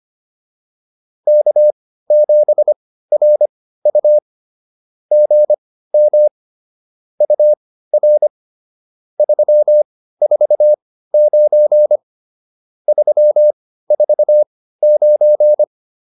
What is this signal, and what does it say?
Morse code, heard as a keyed tone, about 25 wpm, spaced as for 15 wpm, 590 Hz. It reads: K7RU GM UR 349 349